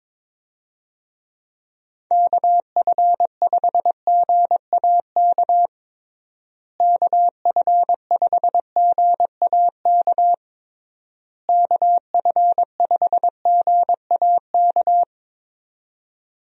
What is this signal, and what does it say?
Morse code, heard as a keyed tone, about 22 wpm, 700 Hz. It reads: KF5GAK KF5GAK KF5GAK
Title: USA Call Signs - 6 Characters - Repeated 3x - 22wpm